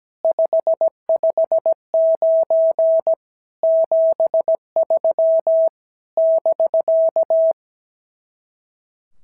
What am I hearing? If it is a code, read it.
Morse code, heard as a keyed tone, about 17 wpm, 650 Hz. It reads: 559 73 <BK>
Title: SOTA QSO Practice 17wpm